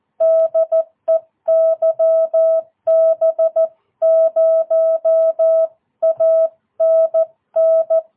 Morse code, heard as a keyed tone, about 14 wpm, 660 Hz. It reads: DE YB0ANN
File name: jh8cbh / 2022cw